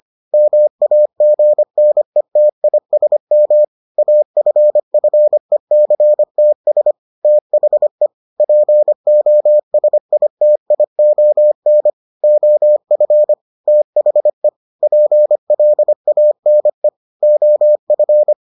Morse code, heard as a keyed tone, about 25 wpm, 600 Hz. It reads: MAGNETISM AFFECTS THE POSITION OF THE PLANE OF